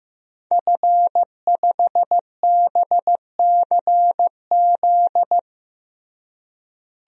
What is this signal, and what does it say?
Morse code, heard as a keyed tone, about 15 wpm, 700 Hz. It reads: F5BCZ
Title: DXCC-100 Call Signs - 5 Characters - Repeated 3x - 15wpm